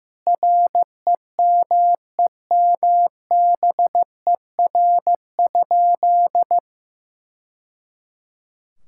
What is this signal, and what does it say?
Morse code, heard as a keyed tone, about 15 wpm, 700 Hz. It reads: REMEMBER?
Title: Sentences from Top 1000 Words - Review Part 6 - 15wpm